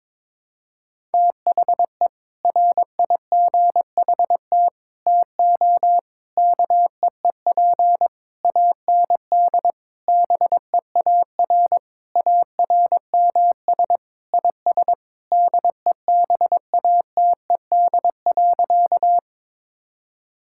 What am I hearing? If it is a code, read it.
Morse code, heard as a keyed tone, about 22 wpm, 700 Hz. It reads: THE RIGHT TO KEEP AND BEAR ARMS IS DEBATED.
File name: Sentences with 7-Word Phrases - 22wpm